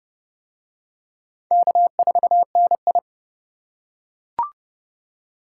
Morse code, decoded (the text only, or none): K4NI